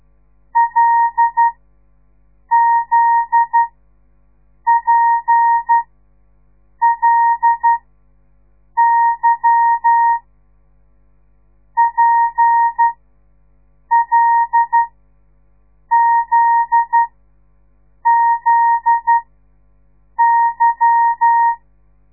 Morse code, heard as a keyed tone, about 12 wpm, spaced as for 6 wpm, 930 Hz. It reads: LZPLY PLZZY
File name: radiomorningstar / morse / les 8